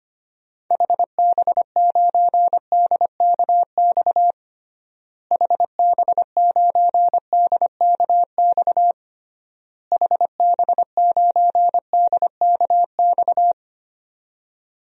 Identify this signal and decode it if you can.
Morse code, heard as a keyed tone, about 25 wpm, 700 Hz. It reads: HB9DKX HB9DKX HB9DKX
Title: DXCC-100 Call Signs - 6 Characters - Repeated 3x - 25wpm